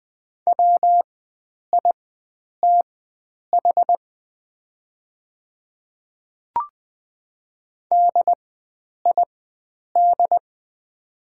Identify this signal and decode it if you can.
Morse code, heard as a keyed tone, about 20 wpm, spaced as for 9 wpm, 700 Hz. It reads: WITH DID
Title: 4x Character Spacing - Top 100 Words - 20wpm